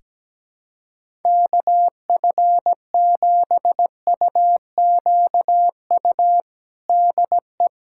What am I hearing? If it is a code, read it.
Morse code, heard as a keyed tone, about 17 wpm, 700 Hz. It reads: KF7UQU DE
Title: USA Callsign DE Callsign - 6 Characters - 17wpm